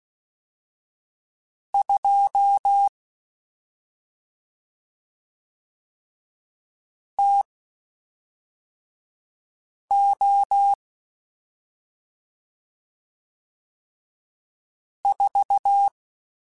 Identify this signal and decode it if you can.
Morse code, heard as a keyed tone, about 16 wpm, spaced as for 3 wpm, 800 Hz. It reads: 2 TO 4
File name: Spaced Characters / Spaced Characters-mp3s / Weather4 14 sp @5